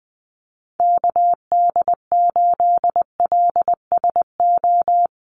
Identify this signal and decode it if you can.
Morse code, heard as a keyed tone, about 20 wpm, 700 Hz. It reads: KD8LSO
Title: USA Call Signs Repeated 2x 20wpm